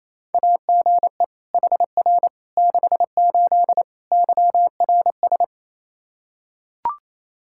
Morse code, heard as a keyed tone, about 28 wpm, 700 Hz. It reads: AGE HR 68 YRS
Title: Practice QSO - Exchange 3 - More Info - 28wpm